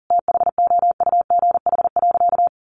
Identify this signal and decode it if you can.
Morse code, heard as a keyed tone, about 37 wpm, 700 Hz. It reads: THOUGH.
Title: Sentences from Top 1500 Words 40wpm